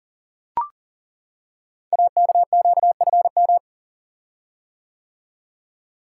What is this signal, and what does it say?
Morse code, heard as a keyed tone, about 38 wpm, 690 Hz. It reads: AKQFM
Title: Code Groups - 5 Letter:Numbers - 40wpm